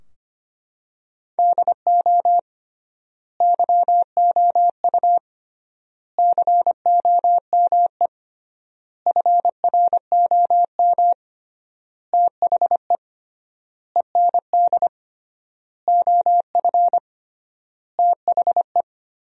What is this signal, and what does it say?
Morse code, heard as a keyed tone, about 25 wpm, 700 Hz. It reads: DO YOU COME FROM THE END OF THE
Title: 3x Word-Spacing - Sentences from Top 200 Words - 25wpm